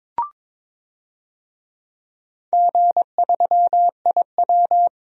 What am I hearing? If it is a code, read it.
Morse code, heard as a keyed tone, about 22 wpm, 700 Hz. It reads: G3IW
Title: DXCC-100 Call Signs - 4 Characters - Repeated 3x - 22wpm